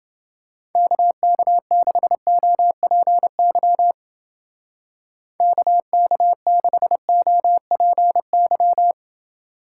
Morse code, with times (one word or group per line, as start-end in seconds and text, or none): 0.8-3.9 KK6OPY
5.4-8.9 KK6OPY